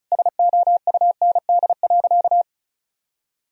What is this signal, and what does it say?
Morse code, heard as a keyed tone, about 34 wpm, 700 Hz. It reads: SOUND.
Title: Sentences from Top 1000 Words - Review Part 1 - 35wpm